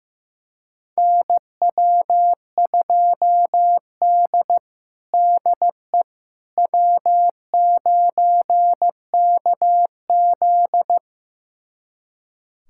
Morse code, heard as a keyed tone, about 15 wpm, 700 Hz. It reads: NW2D DE W9KZ
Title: USA Callsign DE Callsign - 4 Characters - 15wpm